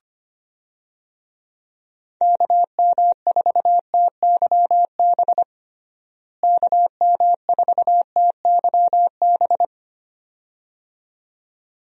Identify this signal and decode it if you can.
Morse code, heard as a keyed tone, about 25 wpm, 700 Hz. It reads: KM4TYB KM4TYB
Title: USA Call Signs - 6 Characters - Repeated 2x - 25wpm